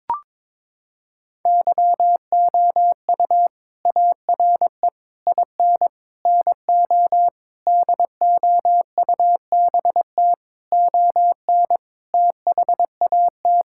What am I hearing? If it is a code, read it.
Morse code, heard as a keyed tone, about 22 wpm, 700 Hz. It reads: YOU ARE IN NO DOUBT ON THAT
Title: Sentences from Top 1600 Words 22wpm